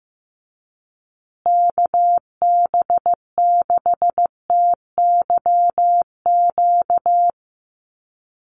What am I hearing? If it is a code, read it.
Morse code, heard as a keyed tone, about 15 wpm, 700 Hz. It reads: KB6TYQ
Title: USA Call Signs Repeated 2x 15wpm